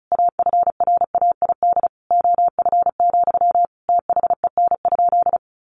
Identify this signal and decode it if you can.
Morse code, heard as a keyed tone, about 34 wpm, 700 Hz. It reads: AFRAID OF, THEN?
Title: Sentences from Top 1500 Words 35wpm